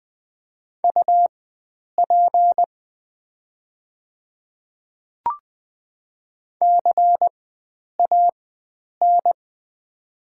Morse code, heard as a keyed tone, about 20 wpm, spaced as for 9 wpm, 700 Hz. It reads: UP CAN
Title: 4x Character Spacing - Top 100 Words - 20wpm